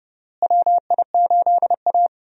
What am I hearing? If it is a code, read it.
Morse code, heard as a keyed tone, about 30 wpm, 700 Hz. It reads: WI8A
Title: USA Call Signs - 4 Characters - Repeated 3x - 30wpm